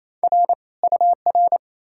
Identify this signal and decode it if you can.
Morse code, heard as a keyed tone, about 37 wpm, 700 Hz. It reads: R UR